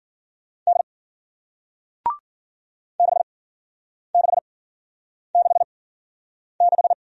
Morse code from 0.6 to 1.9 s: D

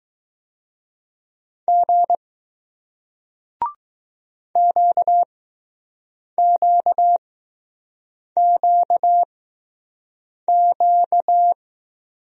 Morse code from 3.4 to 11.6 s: QQQQ

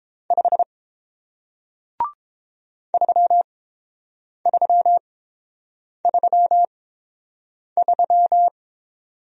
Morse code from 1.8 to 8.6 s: 3333

